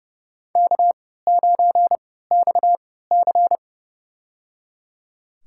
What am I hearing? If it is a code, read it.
Morse code, heard as a keyed tone, about 30 wpm, spaced as for 17 wpm, 700 Hz. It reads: K9XC
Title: USA Call Signs - 4 Characters - 3x Character Spacing - 30wpm